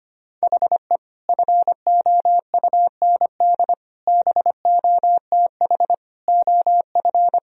Morse code, heard as a keyed tone, about 25 wpm, 700 Hz. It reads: HE FOUND BOTH OF
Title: Sentences from Top 1000 Words - Review Part 1 - 25wpm